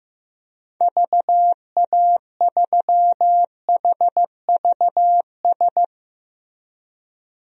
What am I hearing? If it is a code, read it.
Morse code, heard as a keyed tone, about 15 wpm, 700 Hz. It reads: VA3HVS
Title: DXCC-100 Call Signs - 6 Characters - Repeated 2x - 15wpm